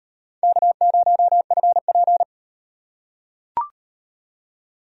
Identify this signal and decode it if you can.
Morse code, heard as a keyed tone, about 35 wpm, 700 Hz. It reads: K0FP